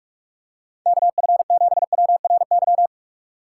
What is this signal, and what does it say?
Morse code, heard as a keyed tone, about 46 wpm, 700 Hz. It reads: KF7WRY
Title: USA Call Signs - 6 Characters - 6x Character Spacing - 45wpm